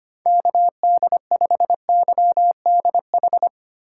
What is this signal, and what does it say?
Morse code, heard as a keyed tone, about 25 wpm, 700 Hz. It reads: KD5YDH
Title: USA Call Signs - 6 Characters - 2x Character Spacing - 25wpm